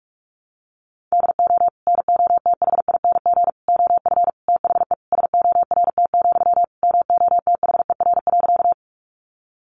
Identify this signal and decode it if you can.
Morse code, heard as a keyed tone, about 44 wpm, 700 Hz. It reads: DO NOTHING OF THE SORT, MOTHER.
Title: Sentences from Top 1300 Words 45wpm